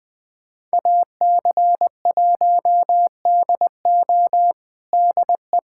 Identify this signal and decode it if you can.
Morse code, heard as a keyed tone, about 19 wpm, 700 Hz. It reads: AC1DO DE
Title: USA Callsign DE Callsign - 5 Characters - 20wpm